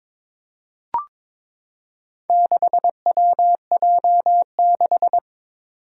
Morse code, heard as a keyed tone, about 22 wpm, 700 Hz. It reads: 6WJ6
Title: Four Challenging Letter-Number Combinations 22wpm